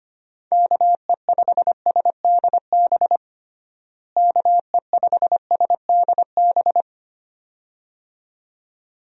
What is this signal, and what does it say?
Morse code, heard as a keyed tone, about 25 wpm, 700 Hz. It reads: KE5SDB KE5SDB
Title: USA Call Signs - 6 Characters - Repeated 2x - 25wpm